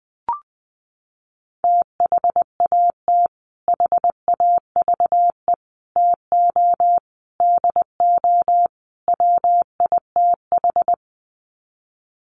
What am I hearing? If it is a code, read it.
Morse code, heard as a keyed tone, about 20 wpm, 700 Hz. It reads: THAT HAVE TO DO WITH
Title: Sets of 5 Words Encore2 20wpm